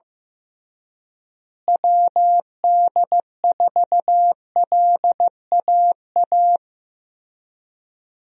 Morse code, heard as a keyed tone, about 15 wpm, 700 Hz. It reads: WD4LAA
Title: USA Call Signs - 6 Characters - Repeated 3x - 15wpm